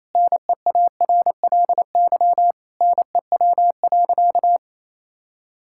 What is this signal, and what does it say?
Morse code, heard as a keyed tone, about 28 wpm, 700 Hz. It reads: NEARLY NEW.
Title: Sentences from Top 1600 Words 28wpm